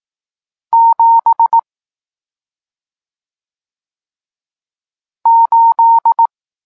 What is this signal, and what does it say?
Morse code, heard as a keyed tone, about 18 wpm, 920 Hz. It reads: 7 8
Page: Tekst 40 – Nauka znaków przestankowych